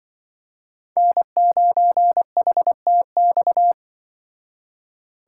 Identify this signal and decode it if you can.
Morse code, heard as a keyed tone, about 24 wpm, 700 Hz. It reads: N9HTX
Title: Speed Racing - USA 1x3 Call Signs 20wpm